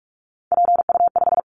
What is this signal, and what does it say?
Morse code, heard as a keyed tone, about 46 wpm, 700 Hz. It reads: PU5
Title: Callsign Prefixes for Top 100 DXCC - 45wpm